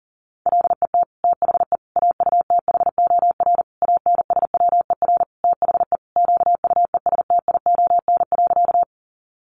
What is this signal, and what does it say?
Morse code, heard as a keyed tone, about 40 wpm, 700 Hz. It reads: LET THE AUTHOR ANSWER THE QUESTION.
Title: Sentences from Top 1200 Words 40wpm